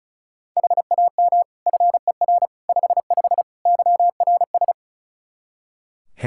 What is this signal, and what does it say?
Morse code, heard as a keyed tone, about 34 wpm, 700 Hz. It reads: HAM FER 55 YRS
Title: K1USN SST QSO Practice 35wpm